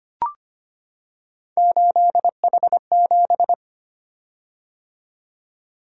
Morse code, heard as a keyed tone, about 25 wpm, 700 Hz. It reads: 8H7
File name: Three Letter-Number Combinations - 25wpm